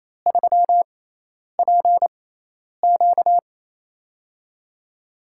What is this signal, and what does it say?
Morse code, heard as a keyed tone, about 28 wpm, spaced as for 10 wpm, 700 Hz. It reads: 3PQ
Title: USA Call Signs - 5 Characters - 6x Character Spacing - 28wpm